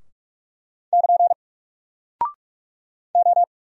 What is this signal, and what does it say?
Morse code, heard as a keyed tone, about 44 wpm, 700 Hz. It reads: <KN> O
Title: Single Character (Letter:Number:Punctuation:Prosign) 45wpm